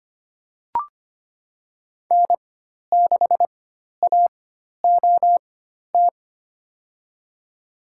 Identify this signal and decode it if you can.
Morse code, heard as a keyed tone, about 25 wpm, spaced as for 12 wpm, 700 Hz. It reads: N6AOT
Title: USA Call Signs - 5 Characters - 4x Character Spacing - 25wpm